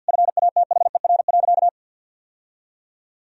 Morse code, heard as a keyed tone, about 50 wpm, 700 Hz. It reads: FATHER.
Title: Sentences from Top 1000 Words - Review Part 2 - 50wpm